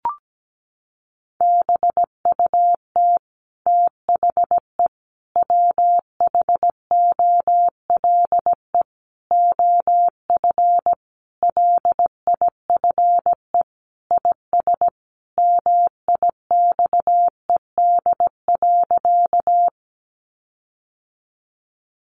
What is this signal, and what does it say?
Morse code, heard as a keyed tone, about 17 wpm, 700 Hz. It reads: BUT THE WHOLE OF LIFE IS MIXED.
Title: Sentences from Top 1500 Words 17wpm